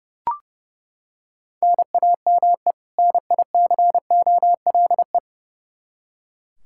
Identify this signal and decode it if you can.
Morse code, heard as a keyed tone, about 28 wpm, 700 Hz. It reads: NAME NICOLE
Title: Practice QSO - Exchange 1 - Beginning - 30wpm